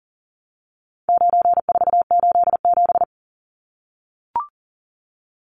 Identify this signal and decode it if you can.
Morse code, heard as a keyed tone, about 40 wpm, 700 Hz. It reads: 9487
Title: Numbers - 4-Digit 40wpm